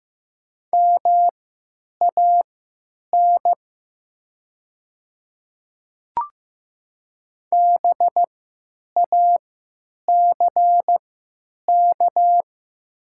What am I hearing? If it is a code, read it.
Morse code, heard as a keyed tone, about 15 wpm, spaced as for 9 wpm, 700 Hz. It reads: MAN BACK
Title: 3x Character Spacing - Top 200 Words - 15wpm